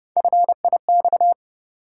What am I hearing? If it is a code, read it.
Morse code, heard as a keyed tone, about 30 wpm, 700 Hz. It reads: FIX
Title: Sentences from Top 1400 Words - New Word Drill - 30wpm